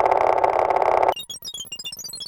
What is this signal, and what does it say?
There is no code.